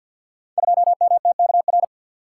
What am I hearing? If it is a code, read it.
Morse code, heard as a keyed tone, about 51 wpm, 700 Hz. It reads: 2MTKR
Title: Code Groups - 5 Letter:Numbers - 50wpm